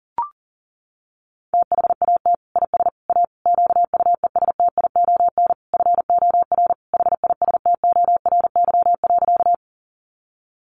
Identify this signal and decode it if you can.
Morse code, heard as a keyed tone, about 37 wpm, 700 Hz. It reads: THAT IS A QUESTION FOR HISTORY.
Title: Sentences from Top 600 Words 40wpm